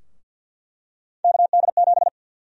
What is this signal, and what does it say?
Morse code, heard as a keyed tone, about 48 wpm, 700 Hz. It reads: KD7
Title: USA Call Sign Prefixes - 50wpm